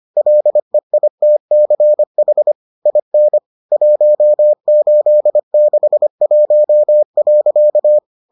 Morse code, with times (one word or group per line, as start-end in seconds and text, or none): 0.2-2.5 LEITCH
2.9-3.4 IN
3.7-8.0 1861.